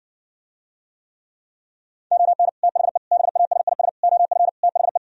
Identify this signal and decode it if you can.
Morse code, heard as a keyed tone, about 70 wpm, 700 Hz. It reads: ON THE BASIS OF THE